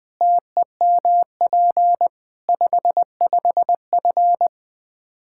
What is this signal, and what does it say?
Morse code, heard as a keyed tone, about 21 wpm, 700 Hz. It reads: TEMP 55F
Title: Practice QSO - Exchange 2 - Info - 20wpm